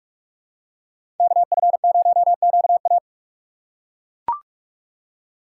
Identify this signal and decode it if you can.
Morse code, heard as a keyed tone, about 42 wpm, 700 Hz. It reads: KF0QA
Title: USA Call Signs - 6x Character Spacing - 45wpm